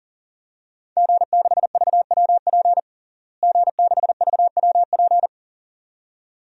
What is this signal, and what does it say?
Morse code, heard as a keyed tone, about 40 wpm, 700 Hz. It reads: G6VWP G6VWP
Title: DXCC-100 Call Signs - 5 Characters - Repeated 2x - 40wpm